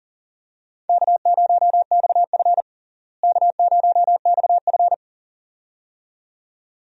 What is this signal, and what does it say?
Morse code, heard as a keyed tone, about 40 wpm, 700 Hz. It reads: K0XF K0XF
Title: USA Call Signs - 4 Characters - Repeated 2x - 40wpm